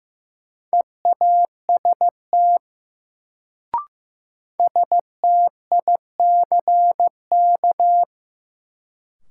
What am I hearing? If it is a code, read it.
Morse code, heard as a keyed tone, about 15 wpm, 700 Hz. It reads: EAST STICK
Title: Sentences from Top 1200 Words - New Word Drill Alt - 15wpm